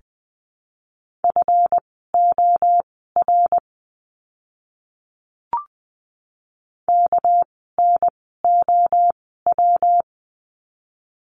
Morse code, heard as a keyed tone, about 20 wpm, spaced as for 14 wpm, 700 Hz. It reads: FOR KNOW
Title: 199 Lesson 32f - Letter k - Words 20wpm